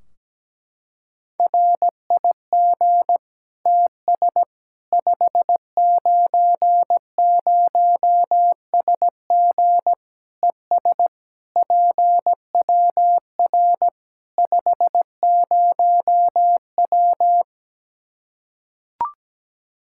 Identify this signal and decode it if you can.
Morse code, heard as a keyed tone, about 17 wpm, 700 Hz. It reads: RIG TS 590SG ES PWR 50W
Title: Practice QSO - Exchange 2 - Info - 17wpm